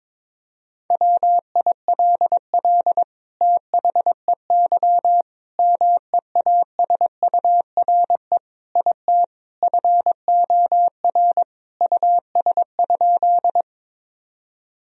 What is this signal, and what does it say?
Morse code, heard as a keyed tone, about 22 wpm, 700 Hz. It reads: WILL THEY MEASURE IT FOR US?
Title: Sentences from Top 1000 Words - Review Part 2 - 22wpm